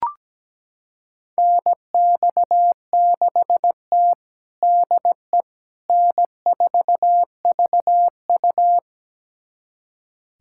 Morse code, heard as a keyed tone, about 17 wpm, 700 Hz. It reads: NX6T DE N4VU
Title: USA Callsign DE Callsign - 4 Characters - 17wpm